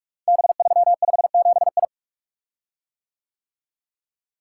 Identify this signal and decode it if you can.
Morse code, heard as a keyed tone, about 44 wpm, 700 Hz. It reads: B357I
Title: Code Groups - 5 Letter:Numbers - 45wpm